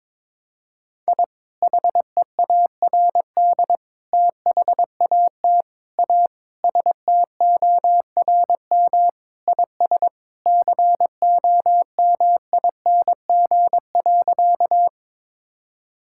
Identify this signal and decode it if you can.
Morse code, heard as a keyed tone, about 22 wpm, 700 Hz. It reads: I HEARD THAT A STORM IS COMING.
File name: practice / 22wpm / Sentences with 2-Word Phrases - 22wpm